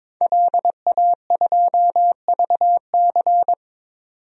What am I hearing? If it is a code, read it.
Morse code, heard as a keyed tone, about 22 wpm, 700 Hz. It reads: LA2VC
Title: DXCC-100 Call Signs - 5 Characters - 22wpm